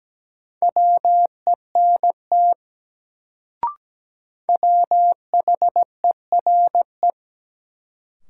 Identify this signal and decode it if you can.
Morse code, heard as a keyed tone, about 17 wpm, 700 Hz. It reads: WENT WHERE
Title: Sets of 2 Words - New Word Drill - 17wpm